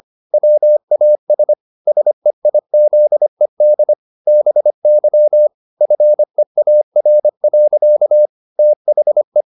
Morse code, heard as a keyed tone, about 25 wpm, 600 Hz. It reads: WAS SEIZED BY FEAR. THE